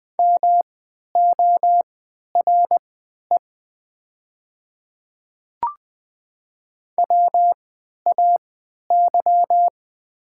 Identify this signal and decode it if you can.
Morse code, heard as a keyed tone, about 20 wpm, spaced as for 11 wpm, 700 Hz. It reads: MORE WAY
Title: 3x Character Spacing - Top 200 Words - 20wpm